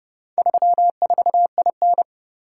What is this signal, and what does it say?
Morse code, heard as a keyed tone, about 26 wpm, 700 Hz. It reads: 34IN